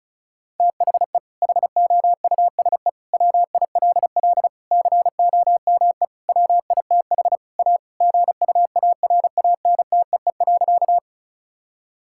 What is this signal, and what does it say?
Morse code, heard as a keyed tone, about 36 wpm, 700 Hz. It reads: THE HOUSE WILL COME WITH A GUARANTEE.
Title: Sentences from Top 1600 Words 35wpm